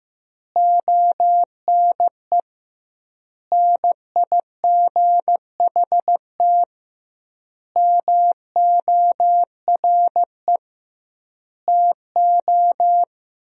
Morse code, heard as a keyed tone, about 16 wpm, 700 Hz. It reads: ONE NIGHT MORE TO
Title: 2x Word-Spacing - Sentences from Top 300 Words - 15wpm